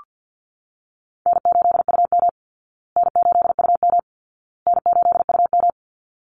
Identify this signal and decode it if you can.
Morse code, heard as a keyed tone, about 48 wpm, 700 Hz. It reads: N8UM N8UM N8UM
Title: Call Signs Repeated 2x - 50wpm